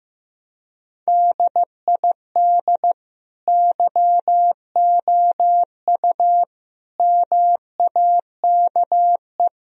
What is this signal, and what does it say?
Morse code, heard as a keyed tone, about 15 wpm, 700 Hz. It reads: DID YOU MAKE
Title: Sentences from Top 1000 Words - Review Part 4 - 15wpm